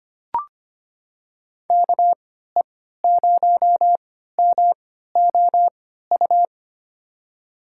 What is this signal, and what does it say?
Morse code, heard as a keyed tone, about 25 wpm, spaced as for 14 wpm, 700 Hz. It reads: KE0MOU
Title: USA Call Signs - 3x Character Spacing - 25wpm